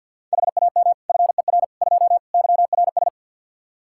Morse code, heard as a keyed tone, about 47 wpm, 700 Hz. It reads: HAM FER 2 YRS
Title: Practice QSO - Exchange 3 - More Info - 50wpm